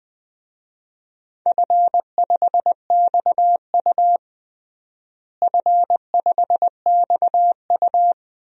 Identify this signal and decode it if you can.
Morse code, heard as a keyed tone, about 20 wpm, 700 Hz. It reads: F5XU F5XU
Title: DXCC-100 Call Signs - 4 Characters - Repeated 2x - 20wpm